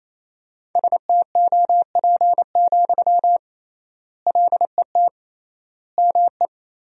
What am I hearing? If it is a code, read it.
Morse code, heard as a keyed tone, about 28 wpm, 700 Hz. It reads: STOP, LET ME
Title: 3x Word-Spacing - Sentences from Top 300 Words - 28wpm